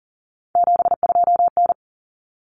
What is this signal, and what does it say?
Morse code, heard as a keyed tone, about 40 wpm, 700 Hz. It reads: 72N
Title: Common Cut Numbers - 3-Digit 40wpm